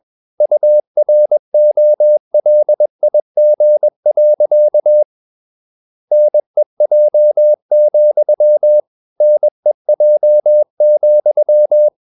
Morse code, heard as a keyed tone, about 21 wpm, 600 Hz. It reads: UROLIG. NEJ, NEJ,
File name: Ferie QRS 21wpm